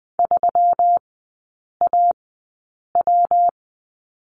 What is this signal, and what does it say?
Morse code, heard as a keyed tone, about 20 wpm, spaced as for 8 wpm, 690 Hz. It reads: 3AW
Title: 212 Lesson 34e - Number 8 - Three Characters 20wpm